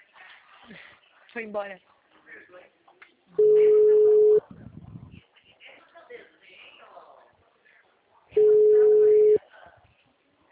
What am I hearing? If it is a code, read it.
Ringback tone (Europe, 425 Hz), with speech.
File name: out- -2012- - - .65952